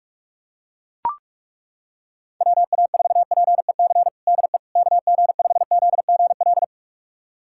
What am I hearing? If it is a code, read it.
Morse code, heard as a keyed tone, about 41 wpm, 700 Hz. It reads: WA4PEC DE KG5ZGL
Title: USA Callsign DE Callsign - 6 Characters - 45wpm